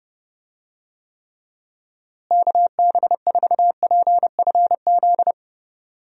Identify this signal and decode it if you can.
Morse code, heard as a keyed tone, about 32 wpm, 700 Hz. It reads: KB4PFZ
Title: USA Call Signs - 6 Characters - Repeated 2x - 30wpm